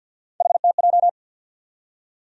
Morse code, heard as a keyed tone, about 52 wpm, 700 Hz. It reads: HTJ
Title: USA Call Sign Postfixes - 3 Letters 50wpm